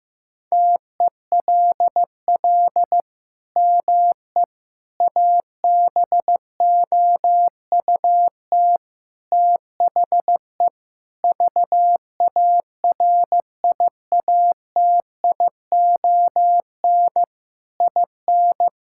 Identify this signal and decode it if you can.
Morse code, heard as a keyed tone, about 15 wpm, 700 Hz. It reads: TELL ME ABOUT THE VARIATION IN
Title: Sentences from Top 1600 Words 15wpm